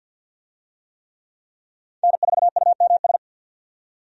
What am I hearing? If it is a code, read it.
Morse code, heard as a keyed tone, about 50 wpm, 700 Hz. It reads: N4UMS
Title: USA Call Signs - 5 Characters - 50wpm